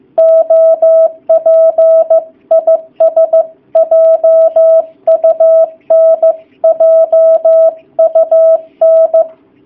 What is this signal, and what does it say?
Morse code, heard as a keyed tone, about 15 wpm, 650 Hz. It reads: OP IS JUN JUN
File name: jh8cbh / 2019cw / sp15- w